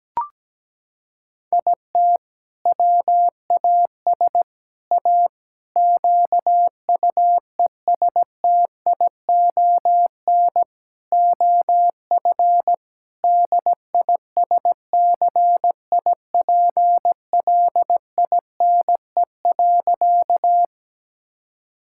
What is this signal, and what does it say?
Morse code, heard as a keyed tone, about 17 wpm, 700 Hz. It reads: IT WAS A QUESTION OF DISCIPLINE.
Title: Sentences from Top 1000 Words - Review Part 6 - 17wpm